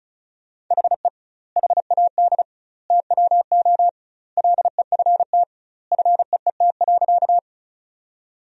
Morse code, heard as a keyed tone, about 36 wpm, 700 Hz. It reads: HE HAD TWO LEFT FEET.
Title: 2x Word-Spacing - Sentences from Top 300 Words - 35wpm